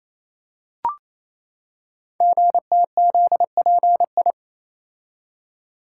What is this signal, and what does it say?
Morse code, heard as a keyed tone, about 29 wpm, 700 Hz. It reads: GTZPI
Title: Code Groups - 5 Letters - 28wpm